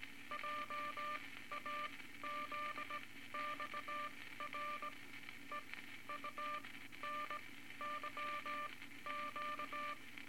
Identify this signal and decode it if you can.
Morse code, heard as a keyed tone, about 18 wpm, spaced as for 13 wpm, 1190 Hz, with noise. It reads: JAZXR EUNYQ
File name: Morse Code Records / 18 A LCM 164 Ten Words Per Minute